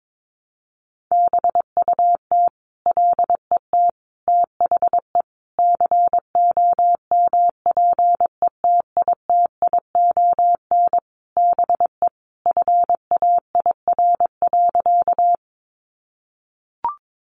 Morse code, heard as a keyed tone, about 22 wpm, 700 Hz. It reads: BUT LET THE COMPETITION BE FAIR.
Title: Sentences from Top 1300 Words 22wpm